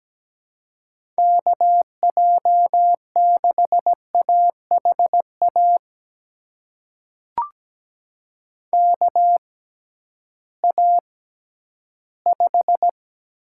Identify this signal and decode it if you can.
Morse code, heard as a keyed tone, about 17 wpm, spaced as for 11 wpm, 700 Hz. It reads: KJ6AHA KA5
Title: USA Call Signs - 6 Characters - 6x Character Spacing - 17wpm